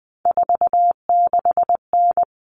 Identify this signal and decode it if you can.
Morse code, heard as a keyed tone, about 20 wpm, 700 Hz. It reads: 46N
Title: Common Cut Numbers - 3-Digit 20wpm